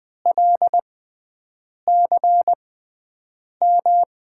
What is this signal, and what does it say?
Morse code, heard as a keyed tone, about 20 wpm, spaced as for 7 wpm, 700 Hz. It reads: LCM
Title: USA Call Signs - 6x Character Spacing - 20wpm